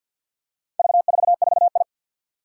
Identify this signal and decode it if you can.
Morse code, heard as a keyed tone, about 53 wpm, 700 Hz. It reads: V44I
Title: Four Challenging Letter-Number Combinations 50wpm